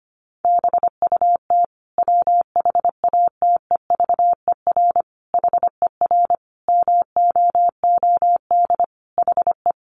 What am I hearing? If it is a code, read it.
Morse code, heard as a keyed tone, about 25 wpm, 700 Hz. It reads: BUT WHATEVER HER MOOD HE
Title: Sentences from Top 1100 Words 25wpm